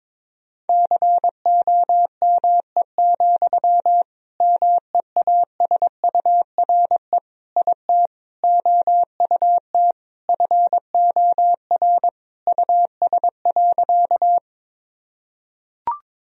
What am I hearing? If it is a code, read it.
Morse code, heard as a keyed tone, about 22 wpm, 700 Hz. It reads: COME, MEASURE IT OUT FOR US.
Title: Sentences from Top 1000 Words - Review Part 2 - 22wpm